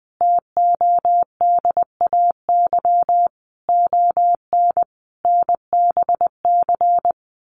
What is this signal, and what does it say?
Morse code, heard as a keyed tone, about 20 wpm, 700 Hz. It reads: TODAY ON NBC
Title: Sets of 5 Words Encore 20wpm